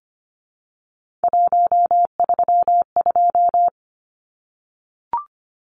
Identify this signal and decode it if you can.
Morse code, heard as a keyed tone, about 25 wpm, 700 Hz. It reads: 132
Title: Numbers - 3-Digit 25wpm